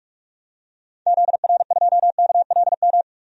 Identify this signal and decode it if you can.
Morse code, heard as a keyed tone, about 44 wpm, 700 Hz. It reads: ZR2KLM
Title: DXCC-100 Call Signs - 6 Characters - Repeated 2x - 45wpm